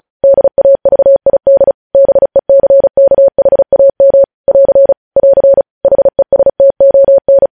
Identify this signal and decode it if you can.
Morse code, heard as a keyed tone, about 36 wpm, 550 Hz. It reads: DAVID BECKHAM <AR> <AR> HESTON
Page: news headlines in morse code at 35 wpm